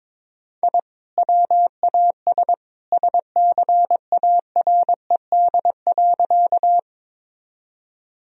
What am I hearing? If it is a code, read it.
Morse code, heard as a keyed tone, about 22 wpm, 700 Hz. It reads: I WAS SCARED.
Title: Sentences from Top 1000 Words - Review Part 4 - 22wpm